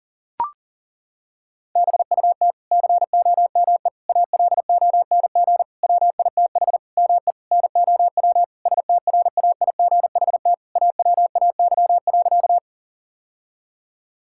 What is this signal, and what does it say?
Morse code, heard as a keyed tone, about 40 wpm, 700 Hz. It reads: BUT COME ALONG WITH ME NOW STRAIGHT AWAY.
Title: Sentences from Top 1000 Words - Review Part 6 - 40wpm